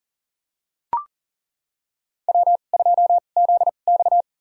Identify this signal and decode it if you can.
Morse code, heard as a keyed tone, about 40 wpm, spaced as for 29 wpm, 700 Hz. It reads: W2ZX
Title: USA Call Signs - 4 Characters - 2x Character Spacing - 40wpm